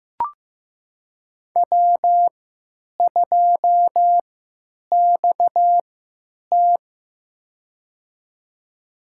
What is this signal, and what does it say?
Morse code, heard as a keyed tone, about 15 wpm, spaced as for 9 wpm, 700 Hz. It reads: W2XT
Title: USA Call Signs - 4 Characters - 3x Character Spacing - 15wpm